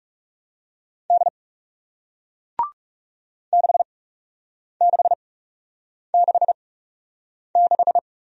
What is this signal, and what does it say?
Morse code, heard as a keyed tone, about 40 wpm, spaced as for 8 wpm, 700 Hz. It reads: D 6666